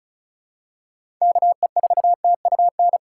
Morse code, heard as a keyed tone, about 34 wpm, 700 Hz. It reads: KE4TUN
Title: USA Call Signs - 6 Characters - 4x Character Spacing - 35wpm